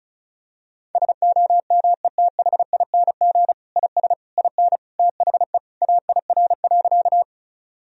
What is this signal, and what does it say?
Morse code, heard as a keyed tone, about 34 wpm, 700 Hz. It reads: SOMETHING IS IN THE AIR.
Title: Sentences from Top 1000 Words - Review Part 1 - 35wpm